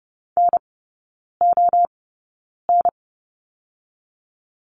Morse code, heard as a keyed tone, about 30 wpm, spaced as for 9 wpm, 700 Hz. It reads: NON
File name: 081 Lesson 14e - Number 2 - Three Characters 30wpm